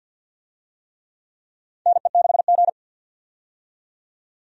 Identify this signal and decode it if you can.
Morse code, heard as a keyed tone, about 52 wpm, 700 Hz. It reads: NE6G
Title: USA Call Signs - 4 Characters - 50wpm